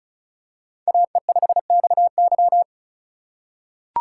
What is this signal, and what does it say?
Morse code, heard as a keyed tone, about 36 wpm, 700 Hz. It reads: AE5XY
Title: USA Call Signs - 5 Characters - 3x Character Spacing - 35wpm